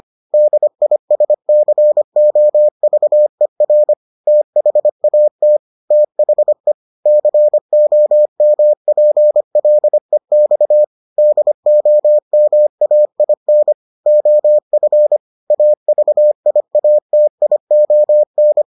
Morse code, heard as a keyed tone, about 25 wpm, 600 Hz. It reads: DISCOVER THAT THE COMPLEX DOMAIN OF AVIATION